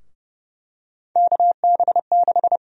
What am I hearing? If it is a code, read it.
Morse code, heard as a keyed tone, about 30 wpm, 700 Hz. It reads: KB6
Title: USA Call Sign Prefixes - Repeated 2x - 30wpm